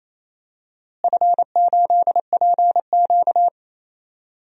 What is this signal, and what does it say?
Morse code, heard as a keyed tone, about 28 wpm, 700 Hz. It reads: F8PQ
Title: DXCC-100 Call Signs - 4 Characters - Repeated 3x - 28wpm